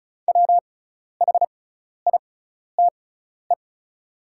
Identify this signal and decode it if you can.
Morse code, heard as a keyed tone, about 36 wpm, spaced as for 12 wpm, 690 Hz. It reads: WHITE
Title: 6x Character Spacing - Top 500 Words - 35wpm